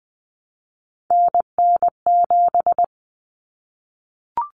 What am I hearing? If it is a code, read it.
Morse code, heard as a keyed tone, about 20 wpm, 700 Hz. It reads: NN7
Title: Common Cut Numbers - 3-Digit 20wpm